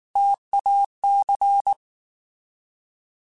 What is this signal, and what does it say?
Morse code, heard as a keyed tone, about 19 wpm, 800 Hz. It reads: TAC
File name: Alphabetical / Alphabetical mp3s / Ts @18